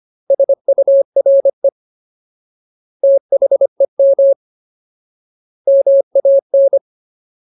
Morse code, heard as a keyed tone, about 25 wpm, 550 Hz. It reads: SURE THEM MAN
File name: Ultimate Word List / 25WPM Ultimate